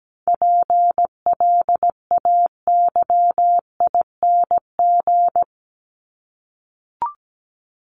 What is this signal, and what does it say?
Morse code, heard as a keyed tone, about 17 wpm, 700 Hz. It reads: PLAYING
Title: ING Suffix - Top 100 Words with Base Words 17wpm